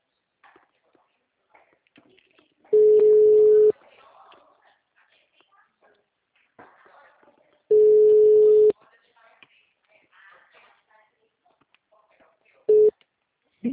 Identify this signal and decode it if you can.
Ringback tone (Europe, 425 Hz), with speech.
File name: out- -2017- - - .142